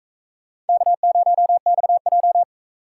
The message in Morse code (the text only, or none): K0XJ